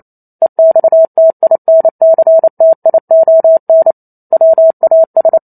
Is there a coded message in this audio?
Yes: Morse code from 0.4 to 5.4 s, heard as a keyed tone, about 28 wpm, 650 Hz. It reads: EXTINCTION WAS